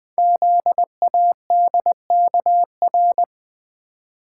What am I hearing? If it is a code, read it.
Morse code, heard as a keyed tone, about 20 wpm, 700 Hz. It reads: ZADKR
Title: Code Groups - 5 Letters - 20wpm